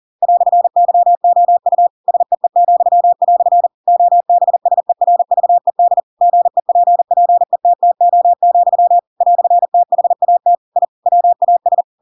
Morse code, heard as a keyed tone, about 40 wpm, 700 Hz. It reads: "YOU SEE," OBSERVED GEPPETTO, "THAT I WAS
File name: Pinocchio / Pinocchio ch7 40 40 ik1hge